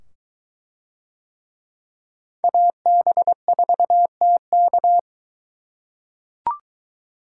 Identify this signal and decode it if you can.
Morse code, heard as a keyed tone, about 22 wpm, 700 Hz. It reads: AB4TK